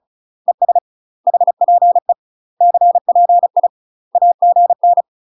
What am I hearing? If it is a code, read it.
Morse code, heard as a keyed tone, about 34 wpm, 700 Hz. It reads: ES HPE CPI AGN
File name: UK MASTER QSOs TRAINING / 35WPM / UK MASTER QSOs 0024